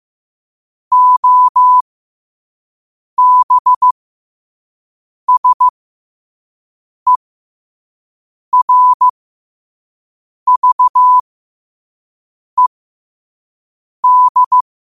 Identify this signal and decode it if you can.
Morse code, heard as a keyed tone, about 15 wpm, spaced as for 5 wpm, 1000 Hz. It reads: OBSERVED
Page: Quotes for Wed, 13 Aug 2025 in Morse Code at 5 words per minute.